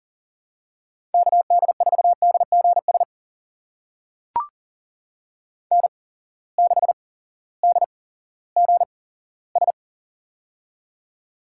Morse code, heard as a keyed tone, about 38 wpm, spaced as for 18 wpm, 690 Hz. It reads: KD4DGS N6DGS
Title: USA Call Signs - 8x Character Spacing - 40wpm